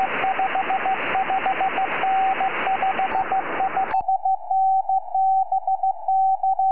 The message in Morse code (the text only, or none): none